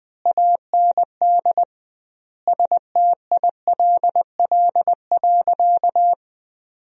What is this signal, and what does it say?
Morse code, heard as a keyed tone, about 20 wpm, 700 Hz. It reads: AND STILL.
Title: 2x Word-Spacing - Sentences from Top 400 Words - 20wpm